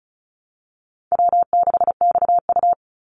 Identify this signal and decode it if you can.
Morse code, heard as a keyed tone, about 34 wpm, 700 Hz. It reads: W6XU
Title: Call Signs Repeated 3x - 35wpm